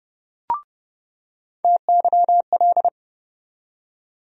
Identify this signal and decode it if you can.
Morse code, heard as a keyed tone, about 30 wpm, 700 Hz. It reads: TYL
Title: USA Call Sign Postfixes - 3 Letters 30wpm